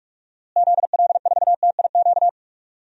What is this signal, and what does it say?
Morse code, heard as a keyed tone, about 46 wpm, 700 Hz. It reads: ZL4TIQ
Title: DXCC-100 Call Signs - 6 Characters - Repeated 3x - 45wpm